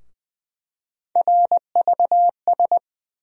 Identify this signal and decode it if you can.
Morse code, heard as a keyed tone, about 20 wpm, 700 Hz. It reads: RVS
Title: USA Call Sign Postfixes - 3 Letters 20wpm